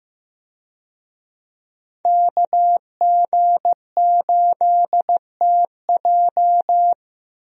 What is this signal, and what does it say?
Morse code, heard as a keyed tone, about 15 wpm, 700 Hz. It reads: KG8TJ
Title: USA Call Signs - 5 Characters - Repeated 2x - 15wpm